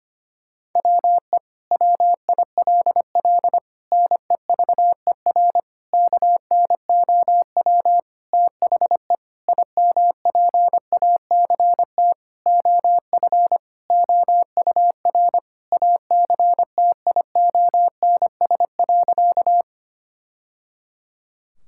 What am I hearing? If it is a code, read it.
Morse code, heard as a keyed tone, about 25 wpm, 700 Hz. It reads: WE WILL NEVER KNOW THE IMPACT OF OUR ACTIONS.
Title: Sentences from Top 1000 Words - Review Part 6 - 25wpm